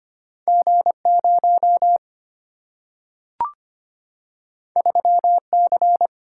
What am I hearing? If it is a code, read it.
Morse code, heard as a keyed tone, about 25 wpm, 690 Hz. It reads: G0 3C
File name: Two Letter-Number Combinations - 25wpm